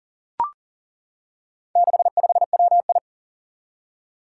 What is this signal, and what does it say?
Morse code, heard as a keyed tone, about 37 wpm, 700 Hz. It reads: 65WI
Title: Four Challenging Letter-Number Combinations 40wpm